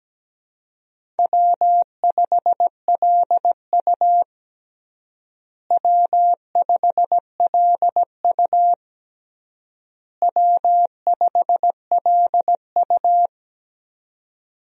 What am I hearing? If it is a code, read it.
Morse code, heard as a keyed tone, about 17 wpm, 700 Hz. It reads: W5LU W5LU W5LU
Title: USA Call Signs - 4 Characters - Repeated 3x - 17wpm